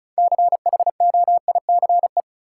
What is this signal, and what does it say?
Morse code, heard as a keyed tone, about 36 wpm, 700 Hz. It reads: CHOICE
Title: Six Letter Words - 35wpm